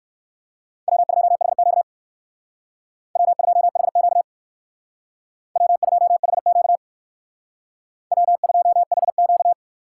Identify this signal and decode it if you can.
Morse code, heard as a keyed tone, about 65 wpm, 700 Hz. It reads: W2HQ W2HQ W2HQ W2HQ